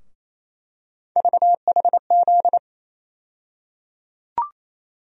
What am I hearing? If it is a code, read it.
Morse code, heard as a keyed tone, about 27 wpm, 700 Hz. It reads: VHZ
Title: USA Call Sign Postfixes - 3 Letters 28wpm